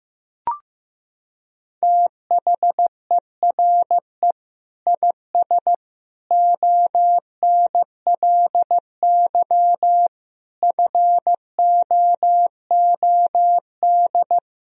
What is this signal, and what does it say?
Morse code, heard as a keyed tone, about 15 wpm, 700 Hz. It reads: THERE IS ONLY FOOD
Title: Sentences from Top 300 Words and Abbreviations 15wpm